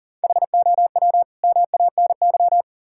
Morse code, heard as a keyed tone, about 37 wpm, 690 Hz. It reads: HOW MANY
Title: Sentences from Top 1000 Words - Review Part 6 - 40wpm